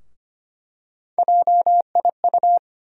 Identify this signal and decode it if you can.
Morse code, heard as a keyed tone, about 25 wpm, 700 Hz. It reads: JIU
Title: USA Call Sign Postfixes - 3 Letters 25wpm